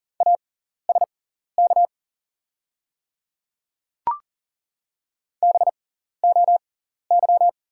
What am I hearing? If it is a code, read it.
Morse code, heard as a keyed tone, about 40 wpm, spaced as for 14 wpm, 690 Hz. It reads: ASK BOY
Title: 6x Character Spacing - Top 200 Words - 40wpm